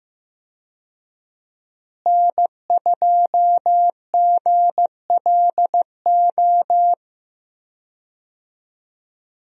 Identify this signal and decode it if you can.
Morse code, heard as a keyed tone, about 16 wpm, 700 Hz. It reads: N2GLO
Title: USA Call Signs - 5 Characters - Repeated 2x - 15wpm